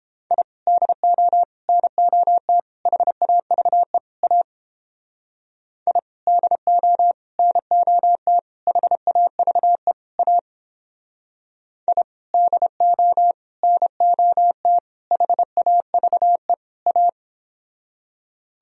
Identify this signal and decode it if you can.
Morse code, heard as a keyed tone, about 29 wpm, 700 Hz. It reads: I DO NOT HAVE A I DO NOT HAVE A I DO NOT HAVE A